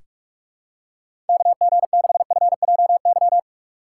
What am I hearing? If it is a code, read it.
Morse code, heard as a keyed tone, about 46 wpm, 700 Hz. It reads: KG6FJY
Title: USA Call Signs - 6x Character Spacing - 45wpm